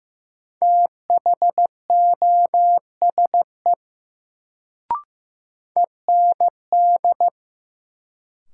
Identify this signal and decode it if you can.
Morse code, heard as a keyed tone, about 15 wpm, 700 Hz. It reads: THOSE END
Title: Sets of 2 Words - New Word Drill - 15wpm